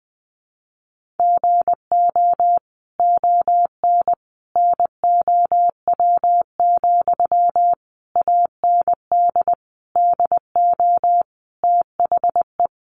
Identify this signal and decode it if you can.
Morse code, heard as a keyed tone, about 20 wpm, 700 Hz. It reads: GO ON NOW, AND DO THE
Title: Sentences from Top 1500 Words 20wpm